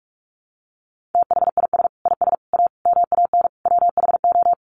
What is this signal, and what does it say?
Morse code, heard as a keyed tone, about 44 wpm, 700 Hz. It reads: THIS IS A MAN WHO
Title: Sets of 5 Words Encore2 45wpm